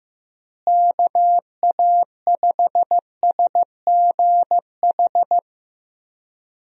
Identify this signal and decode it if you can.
Morse code, heard as a keyed tone, about 15 wpm, 700 Hz. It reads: KA5SGH
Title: USA Call Signs - 6 Characters - Repeated 2x - 15wpm